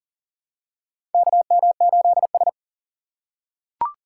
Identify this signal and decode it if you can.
Morse code, heard as a keyed tone, about 40 wpm, 700 Hz. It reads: KM8S
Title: USA Call Signs - 4 Characters - 6x Character Spacing - 40wpm